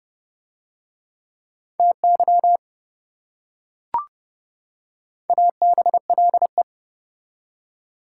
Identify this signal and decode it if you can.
Morse code, heard as a keyed tone, about 28 wpm, 700 Hz. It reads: TY ABLE